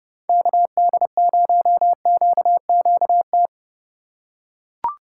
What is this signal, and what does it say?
Morse code, heard as a keyed tone, about 28 wpm, 700 Hz. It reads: KD0QQT